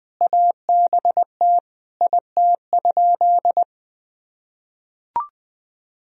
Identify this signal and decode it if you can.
Morse code, heard as a keyed tone, about 20 wpm, 700 Hz. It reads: ABT IT?
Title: Sentences from Top 100 Words and Abbreviations 20wpm